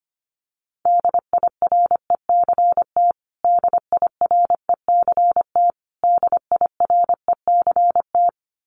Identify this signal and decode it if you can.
Morse code, heard as a keyed tone, about 25 wpm, 700 Hz. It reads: DIRECT DIRECT DIRECT
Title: Top 500 Words Repeated 25wpm